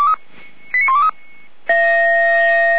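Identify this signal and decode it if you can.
Five-tone selective call (ZVEI), heard as quick runs of five tones, tones of about 70 ms. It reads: 89123
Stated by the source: ZVEI five-tone selective call received with an SDR.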